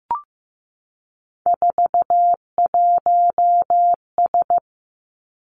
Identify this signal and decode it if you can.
Morse code, heard as a keyed tone, about 15 wpm, 700 Hz. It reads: 41S
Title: Three Challenging Letter-Number Combinations 15wpm